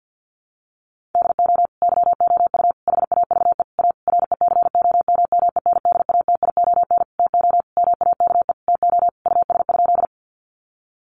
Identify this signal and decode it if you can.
Morse code, heard as a keyed tone, about 51 wpm, 700 Hz. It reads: DO YOU HAVE A RECOMMENDATION TO MAKE TO US?
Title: Sentences from Top 1400 Words 50wpm